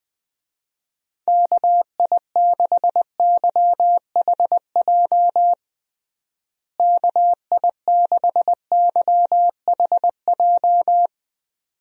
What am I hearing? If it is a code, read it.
Morse code, heard as a keyed tone, about 21 wpm, 700 Hz. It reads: KI6YHJ KI6YHJ
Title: USA Call Signs - 6 Characters - Repeated 2x - 20wpm